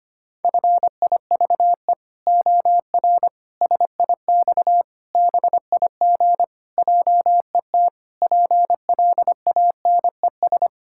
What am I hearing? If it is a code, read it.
Morse code, heard as a keyed tone, about 25 wpm, 700 Hz. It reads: FIVE OR SIX BIG JET PLANES
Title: Pangram Challenge v2 - 25wpm